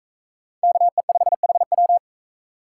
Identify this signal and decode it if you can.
Morse code, heard as a keyed tone, about 40 wpm, 700 Hz. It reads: KE5HW